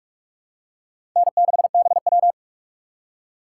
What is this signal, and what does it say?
Morse code, heard as a keyed tone, about 43 wpm, 700 Hz. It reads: N6BW
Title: USA Call Signs - 4 Characters - 2x Character Spacing - 45wpm